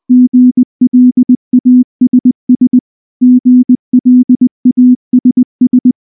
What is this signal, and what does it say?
Morse code, heard as a keyed tone, about 19 wpm, 250 Hz. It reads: GLASS GLASS
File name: morse / morse-training-20WPM-4